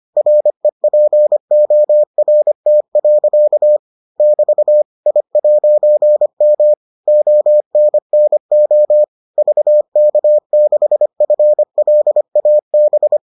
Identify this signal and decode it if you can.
Morse code, heard as a keyed tone, about 25 wpm, 600 Hz. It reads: REPORT. <BT> I'M ONNO VK6FLAB